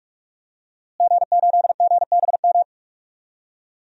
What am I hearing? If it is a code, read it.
Morse code, heard as a keyed tone, about 45 wpm, 700 Hz. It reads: G8GBM
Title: DXCC-100 Call Signs - 5 Characters - Repeated 2x - 45wpm